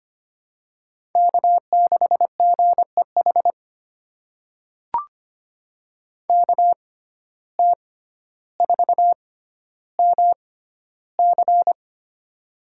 Morse code, heard as a keyed tone, about 25 wpm, spaced as for 13 wpm, 700 Hz. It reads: K6GEH KT4MC
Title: USA Call Signs - 5 Characters - 6x Character Spacing - 25wpm